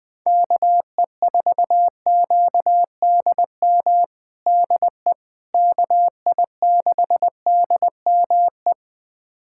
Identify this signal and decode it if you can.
Morse code, heard as a keyed tone, about 20 wpm, 700 Hz. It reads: KE4QDM DE KI6DME
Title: USA Callsign DE Callsign - 6 Characters - 20wpm